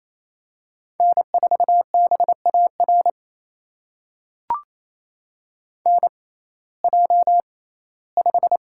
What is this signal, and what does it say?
Morse code, heard as a keyed tone, about 28 wpm, spaced as for 17 wpm, 700 Hz. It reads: N4BAR NJ5
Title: USA Call Signs - 6x Character Spacing - 28wpm